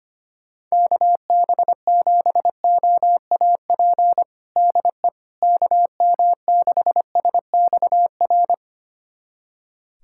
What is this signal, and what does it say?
Morse code, heard as a keyed tone, about 25 wpm, 700 Hz. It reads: KB7OAP DE KM6SXR
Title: USA Callsign DE Callsign - 6 Characters - 25wpm